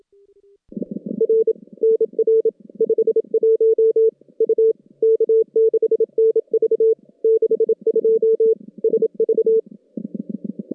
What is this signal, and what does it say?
Morse code, heard as a keyed tone, about 28 wpm, 410 Hz. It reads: R NR 51 U K6NV 62 SV
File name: contests / CW-SS-k5tr- -2200